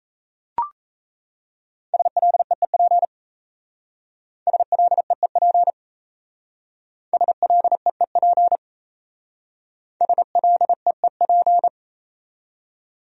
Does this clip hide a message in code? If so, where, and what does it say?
Yes: Morse code from 1.9 to 11.7 s, heard as a keyed tone, about 36 wpm, 700 Hz. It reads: SLEEP SLEEP SLEEP SLEEP